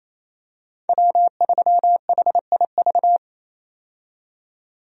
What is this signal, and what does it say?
Morse code, heard as a keyed tone, about 28 wpm, 700 Hz. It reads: W3HIV
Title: USA Call Signs - 5 Characters - 28wpm